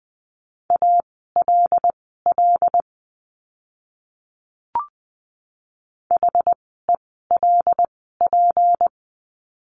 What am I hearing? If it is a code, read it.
Morse code, heard as a keyed tone, about 20 wpm, spaced as for 15 wpm, 700 Hz. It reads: ALL HELP
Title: 237 Lesson 38f - Letter q - Words 20wpm